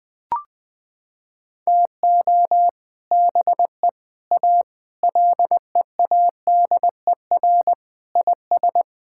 Morse code, heard as a keyed tone, about 20 wpm, 700 Hz. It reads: TO BE A LEADER IS
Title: Sentences with 3-Word Phrases - 20wpm